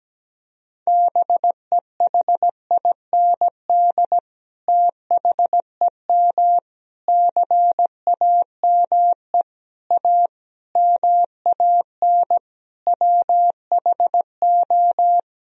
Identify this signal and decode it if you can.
Morse code, heard as a keyed tone, about 17 wpm, 700 Hz. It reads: BEHIND THEM CAME A MAN WHO
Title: Sentences from Top 1000 Words - Review Part 4 - 17wpm